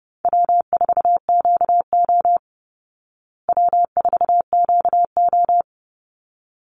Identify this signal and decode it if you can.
Morse code, heard as a keyed tone, about 30 wpm, 700 Hz. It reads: W4QO W4QO
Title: Call Signs Repeated 3x - 30wpm